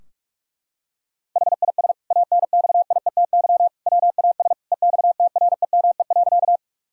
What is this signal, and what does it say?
Morse code, heard as a keyed tone, about 42 wpm, 700 Hz. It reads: HIS ANXIETY WAS EXTREME.
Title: Sentences from Top 1600 Words 45wpm